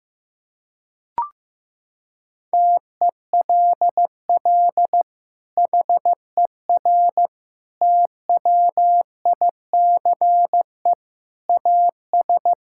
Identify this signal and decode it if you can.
Morse code, heard as a keyed tone, about 15 wpm, 700 Hz. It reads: TELL HER TWICE AS
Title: Sentences from Top 1600 Words 15wpm